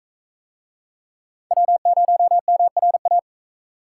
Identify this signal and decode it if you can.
Morse code, heard as a keyed tone, about 39 wpm, 700 Hz. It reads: W0MRA